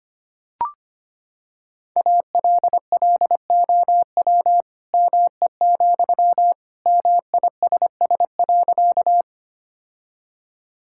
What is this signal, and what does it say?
Morse code, heard as a keyed tone, about 25 wpm, 700 Hz. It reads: ALLOW ME, MISS.
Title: Sentences from Top 1000 Words - Review Part 5 - 25wpm